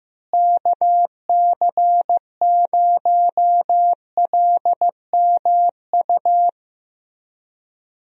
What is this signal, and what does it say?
Morse code, heard as a keyed tone, about 15 wpm, 700 Hz. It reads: KC0LMU
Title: USA Call Signs - 6 Characters - Repeated 3x - 15wpm